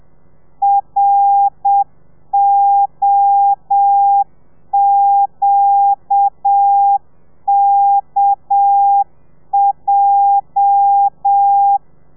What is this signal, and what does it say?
Morse code, heard as a keyed tone, about 7 wpm, 800 Hz. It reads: ROQKJ